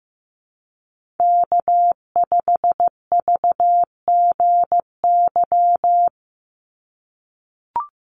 Morse code, heard as a keyed tone, about 15 wpm, 700 Hz. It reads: K5VGY
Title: USA Call Signs 15wpm